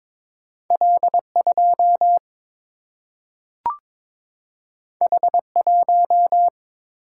Morse code, heard as a keyed tone, about 22 wpm, 700 Hz. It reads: L 2 H 1
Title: Single-Letter Single-Number 22wpm